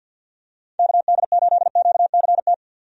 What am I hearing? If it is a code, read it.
Morse code, heard as a keyed tone, about 51 wpm, 700 Hz. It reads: KD8QCT
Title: USA Call Signs - 6 Characters - 50wpm